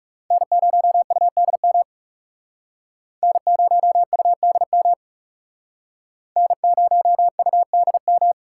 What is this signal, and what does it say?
Morse code, heard as a keyed tone, about 36 wpm, 700 Hz. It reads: N0UDM N0UDM N0UDM
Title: Speed Racing - USA 1x3 Call Signs 30wpm